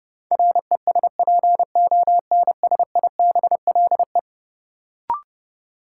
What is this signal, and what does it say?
Morse code, heard as a keyed tone, about 30 wpm, 700 Hz. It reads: RESPONSIBLE
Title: Sentences from Top 1000 Words - Review Part 7 - New Word Drill - 30wpm